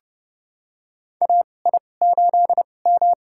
Morse code, heard as a keyed tone, about 30 wpm, spaced as for 22 wpm, 700 Hz. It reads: AI8M
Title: USA Call Signs - 4 Characters - 2x Character Spacing - 30wpm